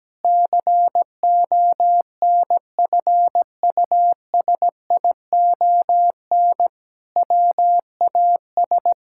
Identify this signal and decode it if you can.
Morse code, heard as a keyed tone, about 17 wpm, 700 Hz. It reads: CONFUSION WAS
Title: Sentences from Top 1600 Words 17wpm